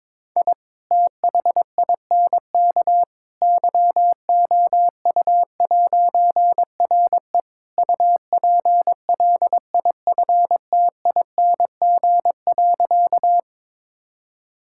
Morse code, heard as a keyed tone, about 22 wpm, 700 Hz. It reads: I THINK YOU'RE UPLIFTING.
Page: morse-code-ninja-practice-22wpm